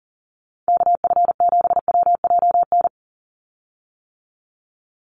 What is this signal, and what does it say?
Morse code, heard as a keyed tone, about 40 wpm, 700 Hz. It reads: KF7WJN
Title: USA Call Signs 40wpm